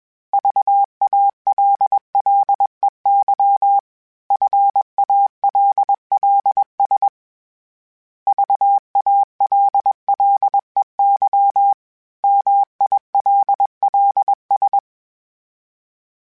Morse code, heard as a keyed tone, about 21 wpm, 800 Hz. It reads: VALLEY FALLS VALLEY MILLS
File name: fromSMs / FromSMs mp3s / Cities@20